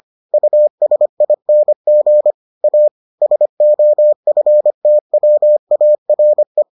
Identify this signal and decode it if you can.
Morse code, heard as a keyed tone, about 25 wpm, 600 Hz. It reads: USING A SOFTWARE